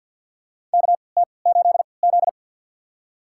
Morse code, heard as a keyed tone, about 50 wpm, spaced as for 28 wpm, 700 Hz. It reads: KT8Z
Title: USA Call Signs - 4 Characters - 3x Character Spacing - 50wpm